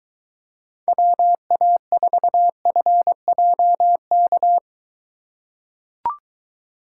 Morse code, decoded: WA4FJK

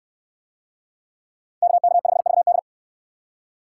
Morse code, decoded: KK5VD

